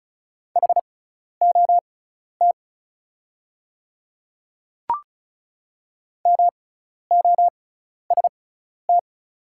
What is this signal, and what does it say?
Morse code, heard as a keyed tone, about 34 wpm, spaced as for 12 wpm, 700 Hz. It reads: HOT MOST
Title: 6x Character Spacing - Top 100 Words - 35wpm